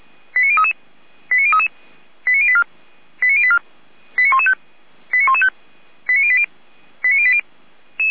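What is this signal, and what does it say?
Five-tone selective call (ZVEI), heard as quick runs of five tones, tones of about 70 ms. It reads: 89033 89033 89084 89084 89115 89115 89080 89080
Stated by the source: ZVEI five-tone call 89117 received on SDR.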